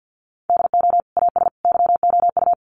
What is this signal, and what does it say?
Morse code, heard as a keyed tone, about 50 wpm, 700 Hz. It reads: DO AS YOU
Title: Sentences from Top 100 Words 50wpm